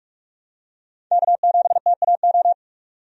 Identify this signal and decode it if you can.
Morse code, heard as a keyed tone, about 47 wpm, 700 Hz. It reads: K7TAO
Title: USA Call Signs - 5 Characters - 45wpm